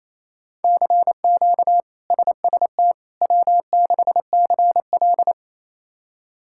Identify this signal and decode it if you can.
Morse code, heard as a keyed tone, about 28 wpm, 700 Hz. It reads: CQ SST W6CL
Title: K1USN SST QSO Practice - 28wpm